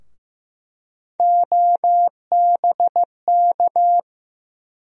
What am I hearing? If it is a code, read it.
Morse code, heard as a keyed tone, about 15 wpm, 700 Hz. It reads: OBK
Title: USA Call Sign Postfixes - 3 Letters 15wpm